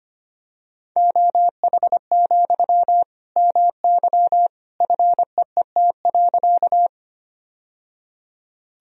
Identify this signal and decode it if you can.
Morse code, heard as a keyed tone, about 25 wpm, 700 Hz. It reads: OH, MY FEET.
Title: Sentences from Top 1000 Words - Review Part 1 - 25wpm